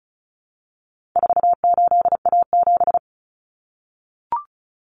Morse code, heard as a keyed tone, about 34 wpm, 700 Hz. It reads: 48A7
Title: Common Cut Numbers - 4-Digit 35wpm